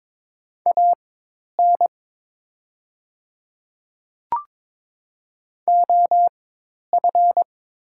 Morse code, heard as a keyed tone, about 22 wpm, spaced as for 10 wpm, 690 Hz. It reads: AN OF
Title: 4x Character Spacing - Top 300 Words - 22wpm